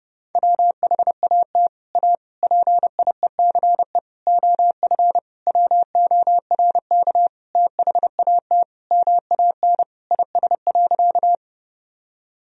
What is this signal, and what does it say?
Morse code, heard as a keyed tone, about 30 wpm, 700 Hz. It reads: WHAT A PIECE OF WORK THAT MAN IS.
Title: Sentences from Top 1000 Words - Review Part 2 - 30wpm